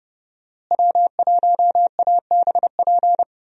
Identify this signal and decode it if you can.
Morse code, heard as a keyed tone, about 32 wpm, 700 Hz. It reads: W1ABP
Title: USA Call Signs - 5 Characters - Repeated 2x - 30wpm